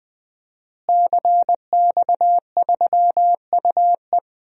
Morse code, heard as a keyed tone, about 21 wpm, 700 Hz. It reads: CX3UE
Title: DXCC-100 Call Signs - 5 Characters - Repeated 3x - 20wpm